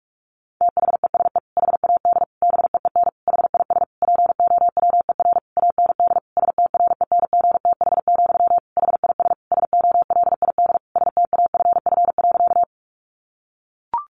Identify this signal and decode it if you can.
Morse code, heard as a keyed tone, about 44 wpm, 700 Hz. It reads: THESE HAD BEEN HIS POWER AND STRENGTH, HIS SOLID STAFF.
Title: Sentences from Top 1500 Words 45wpm